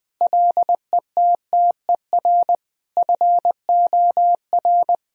Morse code, heard as a keyed tone, about 20 wpm, 700 Hz. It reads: LETTER FOR
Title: Sentences from Top 1000 Words - Review Part 2 - 20wpm